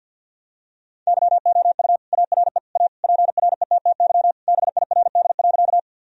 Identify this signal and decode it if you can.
Morse code, heard as a keyed tone, about 50 wpm, 700 Hz. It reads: YOU ARE A PRETTY BIRD.
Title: Sentences from Top 1000 Words - Review Part 2 - 50wpm